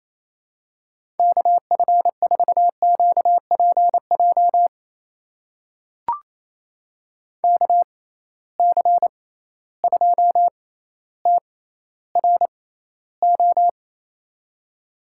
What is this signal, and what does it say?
Morse code, heard as a keyed tone, about 28 wpm, spaced as for 14 wpm, 700 Hz. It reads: KF4QPJ KC2TRO
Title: USA Call Signs - 6 Characters - 6x Character Spacing - 28wpm